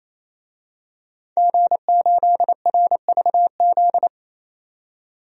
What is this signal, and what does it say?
Morse code, heard as a keyed tone, about 28 wpm, 700 Hz. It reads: G8RVZ
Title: DXCC-100 Call Signs - 5 Characters - 28wpm